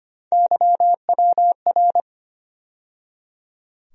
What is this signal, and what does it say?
Morse code, heard as a keyed tone, about 25 wpm, 700 Hz. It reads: YWR
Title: USA Call Sign Postfixes - 3 Letters 25wpm